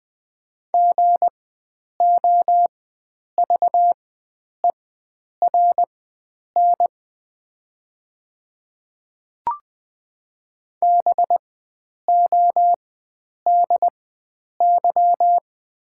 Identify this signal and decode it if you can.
Morse code, heard as a keyed tone, about 20 wpm, spaced as for 9 wpm, 700 Hz. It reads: GOVERN BODY
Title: 4x Character Spacing - Top 500 Words - 20wpm